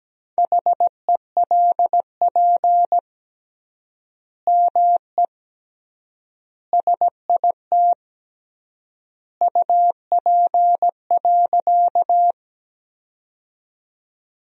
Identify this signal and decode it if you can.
Morse code, heard as a keyed tone, about 17 wpm, 700 Hz. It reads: HELP ME SIT UP.
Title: 3x Word-Spacing - Sentences from Top 500 Words - 17wpm